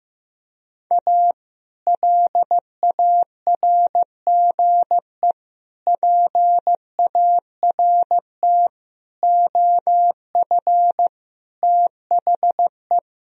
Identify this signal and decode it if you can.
Morse code, heard as a keyed tone, about 16 wpm, 700 Hz. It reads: A LARGE PART OF THE
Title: Sentences from Top 1000 Words - Review Part 1 - 15wpm